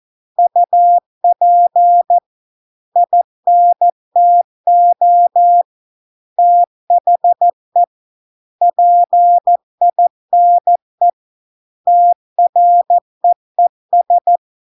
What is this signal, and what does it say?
Morse code, heard as a keyed tone, about 14 wpm, 700 Hz. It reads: UP INTO THE PINE TREES